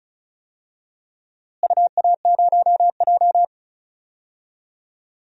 Morse code, heard as a keyed tone, about 36 wpm, 700 Hz. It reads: UA0J
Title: DXCC-100 Call Signs - 4 Characters - Repeated 3x - 35wpm